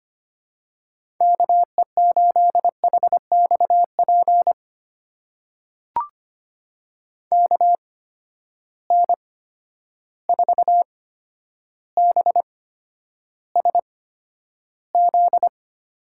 Morse code, heard as a keyed tone, about 25 wpm, spaced as for 11 wpm, 700 Hz. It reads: KE8HXP KN4BSZ
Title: USA Call Signs - 6 Characters - 8x Character Spacing - 25wpm